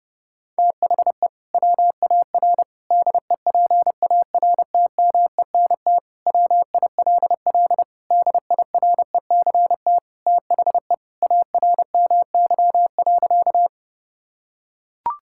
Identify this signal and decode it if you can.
Morse code, heard as a keyed tone, about 30 wpm, 700 Hz. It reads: THE WAR DEPARTMENT WILL DIRECT THE ARMY.
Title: Sentences from Top 1000 Words - Review Part 5 - 30wpm